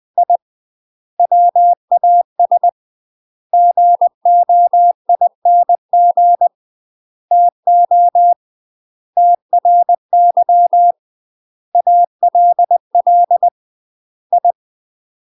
Morse code, heard as a keyed tone, about 20 wpm, 700 Hz. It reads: I WAS GOING TO TRY ALL I
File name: UK MASTER QSOs TRAINING / 20WPM / UK MASTER QSOs 0010